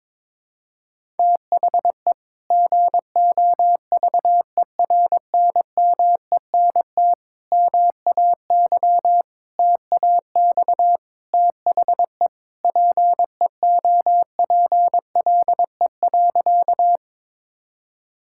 Morse code, heard as a keyed tone, about 22 wpm, 700 Hz. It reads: THE GOVERNMENT MAY TAX THE PEOPLE.
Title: Sentences from Top 1000 Words - Review Part 5 - 22wpm